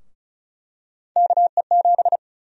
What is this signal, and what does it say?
Morse code, heard as a keyed tone, about 36 wpm, 700 Hz. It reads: KE7
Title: USA Call Sign Prefixes - Repeated 2x - 35wpm